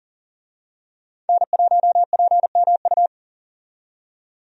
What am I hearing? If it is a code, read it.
Morse code, heard as a keyed tone, about 40 wpm, 700 Hz. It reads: N1PMU
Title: USA Call Signs - 5 Characters - 40wpm